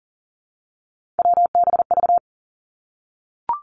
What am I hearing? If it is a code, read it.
Morse code, heard as a keyed tone, about 43 wpm, 700 Hz. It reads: WBV
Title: Three Challenging Letter-Number Combinations 40wpm